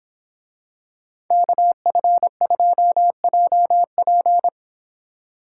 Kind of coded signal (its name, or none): Morse code